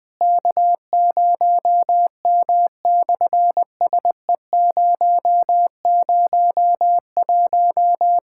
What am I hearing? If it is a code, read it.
Morse code, heard as a keyed tone, about 20 wpm, 700 Hz. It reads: K0M/SE001
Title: SOTA QSO Practice 20wpm